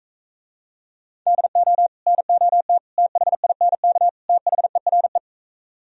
Morse code, heard as a keyed tone, about 44 wpm, 690 Hz. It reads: DO NOT THINK THERE